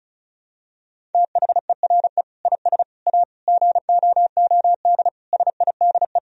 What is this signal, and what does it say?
Morse code, heard as a keyed tone, about 34 wpm, 700 Hz. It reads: THERE IS A GOOD SIDE
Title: Sentences from Top 1000 Words - Review Part 1 - 35wpm